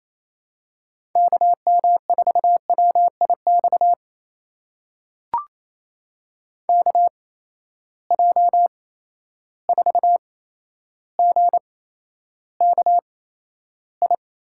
Morse code, heard as a keyed tone, about 27 wpm, spaced as for 12 wpm, 700 Hz. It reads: KM4WIX KJ4GKI
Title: USA Call Signs - 6 Characters - 8x Character Spacing - 28wpm